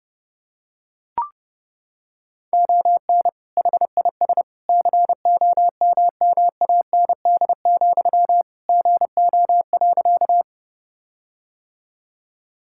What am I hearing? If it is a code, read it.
Morse code, heard as a keyed tone, about 30 wpm, 700 Hz. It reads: ON HIS COMMAND, GO.
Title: Sentences from Top 1600 Words 30wpm